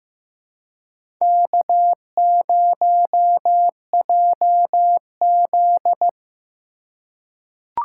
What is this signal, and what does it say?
Morse code, heard as a keyed tone, about 15 wpm, 700 Hz. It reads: K0JZ
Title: USA Call Signs - 4 Characters - 3x Character Spacing - 15wpm